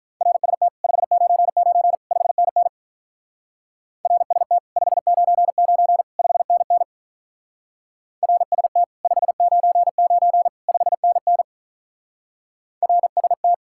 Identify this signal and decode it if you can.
Morse code, heard as a keyed tone, about 43 wpm, 700 Hz. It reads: RST 599 5NN RST 599 5NN RST 599 5NN RST
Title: Speed-Racing Common QSO Phrases - 35wpm